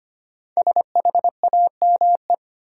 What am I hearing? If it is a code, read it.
Morse code, heard as a keyed tone, about 25 wpm, 700 Hz. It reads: SHAME
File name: 5 Letter Words - First 3 Letters Given - 25wpm